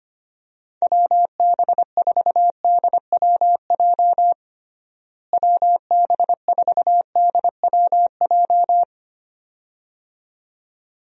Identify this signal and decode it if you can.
Morse code, heard as a keyed tone, about 25 wpm, 700 Hz. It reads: WB4DWJ WB4DWJ
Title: USA Call Signs - 6 Characters - Repeated 2x - 25wpm